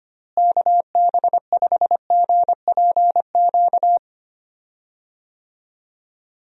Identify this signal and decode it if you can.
Morse code, heard as a keyed tone, about 25 wpm, 700 Hz. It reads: KB5GPQ
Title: USA Call Signs - 6 Characters - Repeated 3x - 25wpm